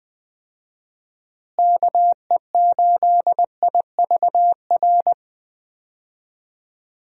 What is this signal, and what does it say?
Morse code, heard as a keyed tone, about 20 wpm, 700 Hz. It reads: KE8IVR
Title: USA Call Signs - 6 Characters - Repeated 2x - 20wpm